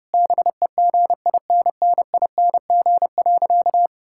Morse code, heard as a keyed tone, about 30 wpm, 700 Hz. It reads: BEGINNING.
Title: Sentences from Top 1000 Words - Review Part 5 - 30wpm